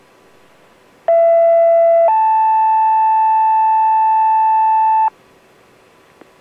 Two-tone page, heard as a one-second tone, then a three-second tone, about 670 then 880 Hz. It is heard from 1.1 to 5.1 s.